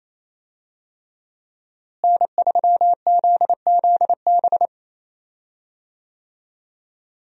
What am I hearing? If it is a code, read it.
Morse code, heard as a keyed tone, about 27 wpm, 700 Hz. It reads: N3ZZB
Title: USA Call Signs - 5 Characters - Repeated 3x - 28wpm